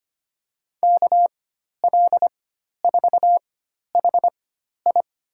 Morse code, heard as a keyed tone, about 25 wpm, spaced as for 12 wpm, 700 Hz. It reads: KL4HI
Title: USA Call Signs - 4x Character Spacing - 25wpm